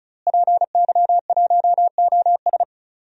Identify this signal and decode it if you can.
Morse code, heard as a keyed tone, about 35 wpm, 700 Hz. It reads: PY1OS
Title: DXCC-100 Call Signs - 5 Characters - Repeated 3x - 35wpm